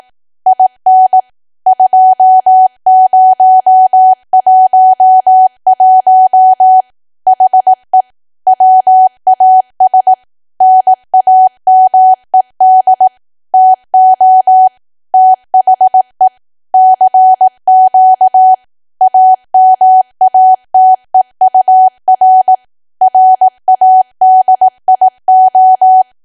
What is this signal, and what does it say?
Morse code, heard as a keyed tone, about 18 wpm, 750 Hz. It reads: IN 2011 HE WAS NAMED TO THE CQ AMATEUR RADIO
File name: Archive / 18 WPM / 18WPM